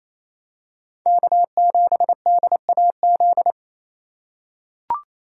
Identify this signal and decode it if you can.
Morse code, heard as a keyed tone, about 28 wpm, 700 Hz. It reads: K7DAZ
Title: USA Call Signs - 5 Characters - 2x Character Spacing - 28wpm